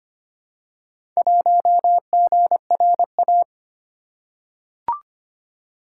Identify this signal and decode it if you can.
Morse code, heard as a keyed tone, about 25 wpm, 700 Hz. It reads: 1GRA
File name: practice / 25wpm / Four Letter-Number Combinations - 25wpm